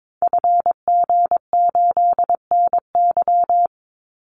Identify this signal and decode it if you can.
Morse code, heard as a keyed tone, about 22 wpm, 700 Hz. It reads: FG8NY
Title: Call Signs Repeated 2x - 22wpm